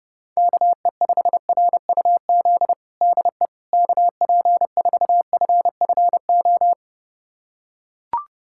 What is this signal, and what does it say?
Morse code, heard as a keyed tone, about 30 wpm, 700 Hz. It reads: KE5RUZ DE KP4FFO
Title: USA Callsign DE Callsign - 6 Characters - 30wpm